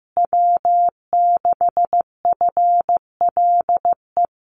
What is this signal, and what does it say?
Morse code, heard as a keyed tone, about 15 wpm, 700 Hz. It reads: W6FLE
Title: Common QSO Phrases Encore 15wpm